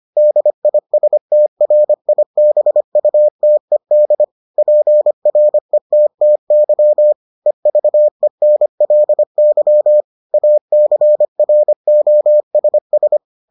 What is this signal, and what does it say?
Morse code, heard as a keyed tone, about 25 wpm, 600 Hz. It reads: DISTRIBUTED PRETTY EVENLY ACROSS